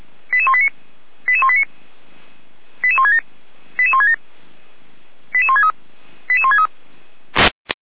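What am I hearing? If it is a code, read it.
Five-tone selective call (ZVEI), heard as quick runs of five tones, tones of about 70 ms. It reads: 88189 88189 88167 88167 88163 88163
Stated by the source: ZVEI five-tone selective call received via SDR, on the audio server of Feuerwehr Orlamünde (fire brigade).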